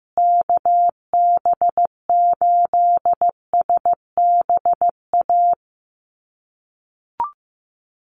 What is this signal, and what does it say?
Morse code, heard as a keyed tone, about 14 wpm, 700 Hz. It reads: KB8SBA
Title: USA Call Signs Repeated 3x 15wpm